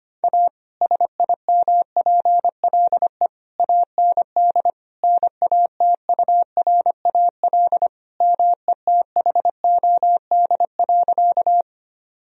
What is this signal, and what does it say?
Morse code, heard as a keyed tone, about 25 wpm, 700 Hz. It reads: A SIMPLE AND NATURAL METHOD.
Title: Sentences from Top 1000 Words - Review Part 4 - 25wpm